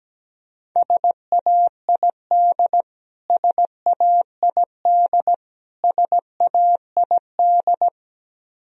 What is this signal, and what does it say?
Morse code, heard as a keyed tone, about 17 wpm, 700 Hz. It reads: SAID SAID SAID
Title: Top 100 Words Repeated 17wpm